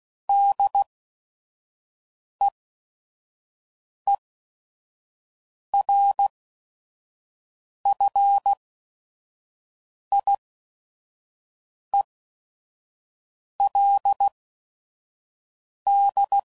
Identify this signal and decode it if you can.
Morse code, heard as a keyed tone, about 16 wpm, spaced as for 5 wpm, 800 Hz. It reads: DEERFIELD
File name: CW Code Groups @5WPM / fromSM's / Cities